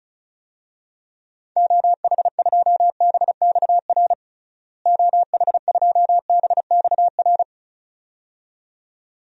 Morse code, heard as a keyed tone, about 34 wpm, 700 Hz. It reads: OH2BXR OH2BXR
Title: DXCC-100 Call Signs - 6 Characters - Repeated 2x - 35wpm